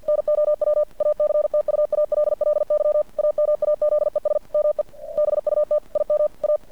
Morse code, heard as a keyed tone, about 50 wpm, 600 Hz. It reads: NOW ACTUALLY AMAZES ME BUT IM A
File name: archived / CRSnet